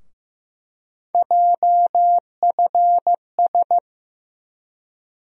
Morse code, heard as a keyed tone, about 15 wpm, 700 Hz. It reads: JFS
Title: USA Call Sign Postfixes - 3 Letters 15wpm